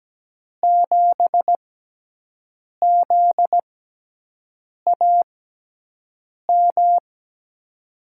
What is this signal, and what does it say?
Morse code, heard as a keyed tone, about 17 wpm, spaced as for 6 wpm, 700 Hz. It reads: 7ZAM
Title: USA Call Signs - 5 Characters - 6x Character Spacing - 17wpm